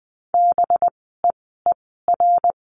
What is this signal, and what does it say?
Morse code, heard as a keyed tone, about 20 wpm, spaced as for 15 wpm, 700 Hz. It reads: BEER
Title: 185 Lesson 30f - Letter b - Words 20wpm